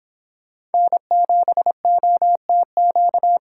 Morse code, heard as a keyed tone, about 30 wpm, 700 Hz. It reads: N7OTQ